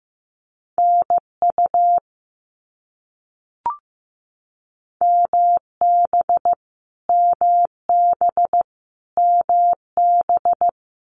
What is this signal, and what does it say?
Morse code, heard as a keyed tone, about 15 wpm, 700 Hz. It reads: NU MB MB MB
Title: Canadian Provinces and Territories - Repeated 3x - 15wpm